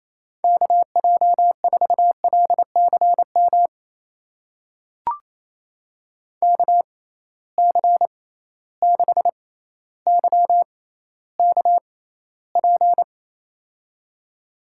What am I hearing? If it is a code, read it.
Morse code, heard as a keyed tone, about 28 wpm, spaced as for 14 wpm, 700 Hz. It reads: KJ4LCM KC6YKP
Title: USA Call Signs - 6x Character Spacing - 28wpm